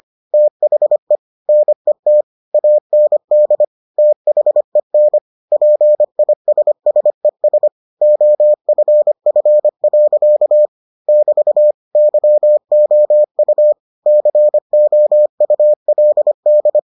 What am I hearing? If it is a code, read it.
Morse code, heard as a keyed tone, about 25 wpm, 600 Hz. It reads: THE NET AND THEN PISSES OFF. <BT> YOU COULD